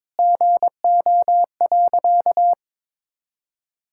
Morse code, heard as a keyed tone, about 22 wpm, 700 Hz. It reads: GO.
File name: practice / 22wpm / Sentences with 4-Word Phrases - 22wpm